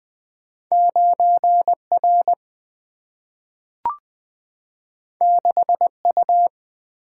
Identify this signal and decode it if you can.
Morse code, heard as a keyed tone, about 20 wpm, 700 Hz. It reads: 9 R 6 U
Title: Single-Letter Single-Number 20wpm